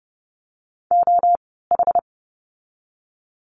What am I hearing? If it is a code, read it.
Morse code, heard as a keyed tone, about 30 wpm, spaced as for 17 wpm, 700 Hz. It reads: OH
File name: Lesson 16f - Letter c - Words 30wpm